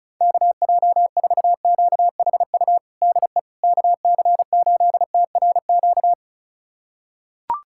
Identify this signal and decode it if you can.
Morse code, heard as a keyed tone, about 34 wpm, 700 Hz. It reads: KJ4QHU DE KC8TRQ
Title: USA Callsign DE Callsign 35wpm